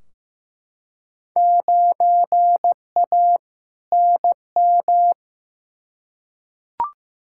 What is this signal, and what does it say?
Morse code, heard as a keyed tone, about 15 wpm, 700 Hz. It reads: 9A NM
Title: Field Day Exchanges Encore - 15wpm